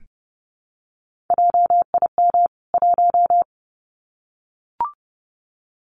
Morse code, heard as a keyed tone, about 30 wpm, 700 Hz. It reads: JIM 1
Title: CWT Exchange 30wpm